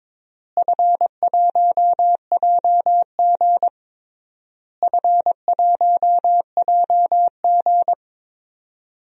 Morse code, heard as a keyed tone, about 22 wpm, 700 Hz. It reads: F1JG F1JG
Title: DXCC-100 Call Signs - 4 Characters - Repeated 3x - 22wpm